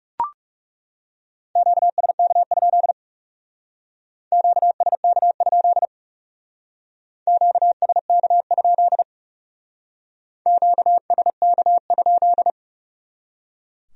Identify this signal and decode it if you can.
Morse code, heard as a keyed tone, about 38 wpm, 700 Hz. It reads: QSK? QSK? QSK? QSK?